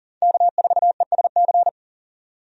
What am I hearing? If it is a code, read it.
Morse code, heard as a keyed tone, about 40 wpm, 700 Hz. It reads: K4ESC
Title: USA Call Signs - 5 Characters - 40wpm